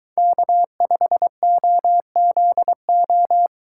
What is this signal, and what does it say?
Morse code, heard as a keyed tone, about 23 wpm, 700 Hz. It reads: K5OZO